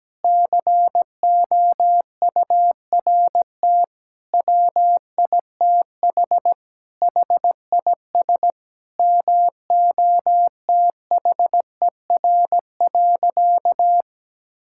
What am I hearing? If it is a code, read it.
Morse code, heard as a keyed tone, about 17 wpm, 700 Hz. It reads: COURT WITH HIS MOTHER.
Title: Sentences from Top 1600 Words 17wpm